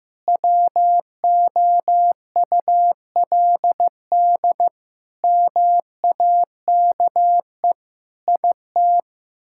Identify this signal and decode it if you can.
Morse code, heard as a keyed tone, about 15 wpm, 700 Hz. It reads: WOULD MAKE IT
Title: Sentences from Top 1000 Words - Review Part 6 - 15wpm